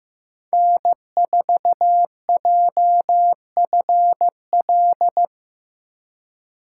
Morse code, heard as a keyed tone, about 15 wpm, 700 Hz. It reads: N4JFL
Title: USA Call Signs - 6x Character Spacing - 15wpm